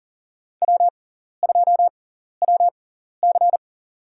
Morse code, heard as a keyed tone, about 40 wpm, spaced as for 14 wpm, 700 Hz. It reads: W2WC
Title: USA Call Signs - 6x Character Spacing - 40wpm